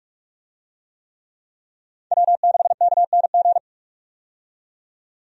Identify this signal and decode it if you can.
Morse code, heard as a keyed tone, about 46 wpm, 700 Hz. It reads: W6KNG
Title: USA Call Signs - 5 Characters - 45wpm